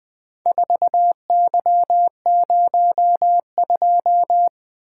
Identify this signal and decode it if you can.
Morse code, heard as a keyed tone, about 20 wpm, 700 Hz. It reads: 4Y02
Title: Four Letter-Number Combinations - 20wpm